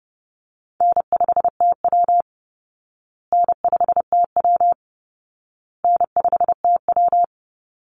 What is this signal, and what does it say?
Morse code, heard as a keyed tone, about 28 wpm, 700 Hz. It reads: N5TW N5TW N5TW
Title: Easy Call Signs Repeated 3x - 30wpm